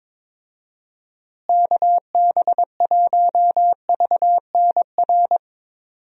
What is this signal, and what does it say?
Morse code, heard as a keyed tone, about 22 wpm, 700 Hz. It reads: KB1VNR
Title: USA Call Signs - 6 Characters - Repeated 3x - 22wpm